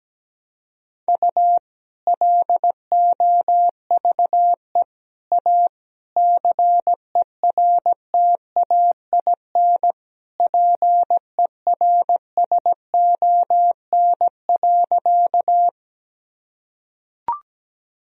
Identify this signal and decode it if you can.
Morse code, heard as a keyed tone, about 17 wpm, 700 Hz. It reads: U LOVE A CERTAIN PERSON.
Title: Sentences from Top 500 Words and Abbreviations 17wpm